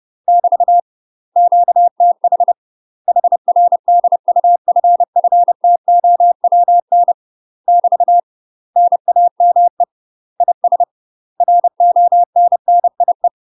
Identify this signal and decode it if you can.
Morse code, heard as a keyed tone, about 30 wpm, 700 Hz. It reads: <BT> QTH HR DUFFTOWN <BT> NAME IS RONNIE
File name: UK MASTER QSOs TRAINING / 30WPM / UK MASTER QSOs 0036